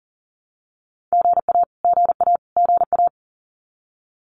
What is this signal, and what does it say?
Morse code, heard as a keyed tone, about 40 wpm, 700 Hz. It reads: GA GA GA
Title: Abbreviations Repeated 3x - 40wpm